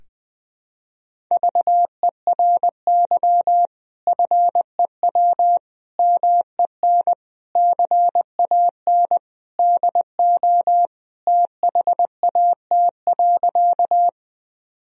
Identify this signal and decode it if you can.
Morse code, heard as a keyed tone, about 20 wpm, 700 Hz. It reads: VERY FEW MEN CAN DO THAT.
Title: Sentences from Top 1000 Words - Review Part 1 - 20wpm